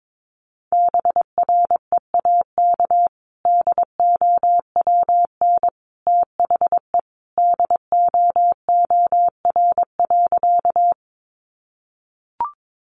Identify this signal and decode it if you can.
Morse code, heard as a keyed tone, about 22 wpm, 700 Hz. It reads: BREAK DOWN THE DOOR.
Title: Sentences from Top 900 Words 22wpm